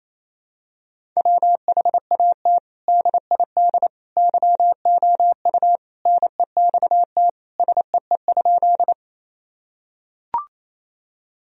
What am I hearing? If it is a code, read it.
Morse code, heard as a keyed tone, about 28 wpm, 700 Hz. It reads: WHAT DID YOU NEXT SEE?
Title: Sentences from Top 1000 Words - Review Part 1 - 28wpm